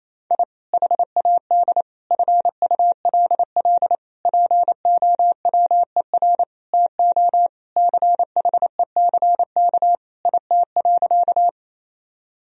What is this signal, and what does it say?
Morse code, heard as a keyed tone, about 28 wpm, 700 Hz. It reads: I HAD FULL POWER TO CHECK IT.
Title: Sentences from Top 1000 Words - Review Part 3 - 28wpm